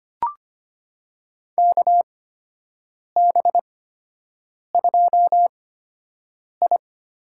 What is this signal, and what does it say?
Morse code, heard as a keyed tone, about 25 wpm, spaced as for 7 wpm, 700 Hz. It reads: KB2I
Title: USA Call Signs - 5 Characters - 8x Character Spacing - 25wpm